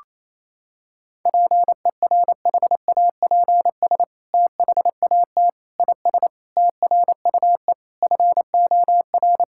Morse code, heard as a keyed tone, about 28 wpm, 700 Hz. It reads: PERHAPS THAT IS TRUE FOR
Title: Sentences from Top 1000 Words - Review Part 5 - 28wpm